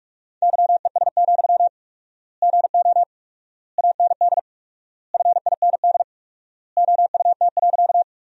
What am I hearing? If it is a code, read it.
Morse code, heard as a keyed tone, about 43 wpm, 700 Hz. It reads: YES, GO AND FIND OUT.
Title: 4x Word-Spacing - Sentences from Top 500 Words - 45wpm